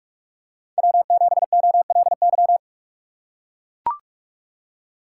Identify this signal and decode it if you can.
Morse code, heard as a keyed tone, about 45 wpm, 690 Hz. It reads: W7OLY
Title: USA Call Signs - 4x Character Spacing - 45wpm